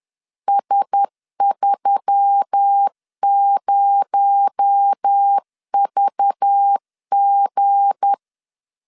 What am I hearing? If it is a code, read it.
Morse code, heard as a keyed tone, about 10 wpm, 800 Hz. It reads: S30VG
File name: LEZIONI / 8 YJ45 / 8 Y 10wpm 2615